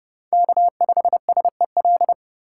Morse code, heard as a keyed tone, about 30 wpm, 700 Hz. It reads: K5SEL
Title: USA Call Signs - 5 Characters - 4x Character Spacing - 30wpm